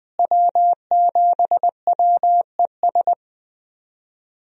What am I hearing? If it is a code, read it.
Morse code, heard as a keyed tone, about 21 wpm, 700 Hz. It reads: W7WES
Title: USA Call Signs - 5 Characters - Repeated 2x - 20wpm